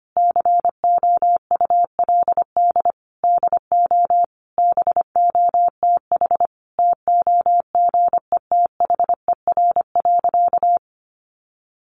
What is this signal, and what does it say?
Morse code, heard as a keyed tone, about 25 wpm, 700 Hz. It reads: COULD DO BOTH TOGETHER.
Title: Sentences from Top 1300 Words 25wpm